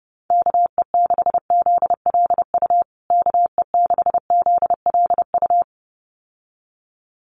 Morse code, heard as a keyed tone, about 30 wpm, 700 Hz. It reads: KE6ZLU KE6ZLU
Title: USA Call Signs Repeated 2x 30wpm